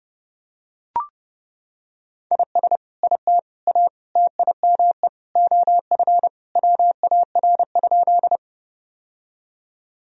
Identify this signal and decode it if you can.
Morse code, heard as a keyed tone, about 32 wpm, 700 Hz. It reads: IS IT A TIME OF WAR?
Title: Sentences from Top 1000 Words - Review Part 2 - 30wpm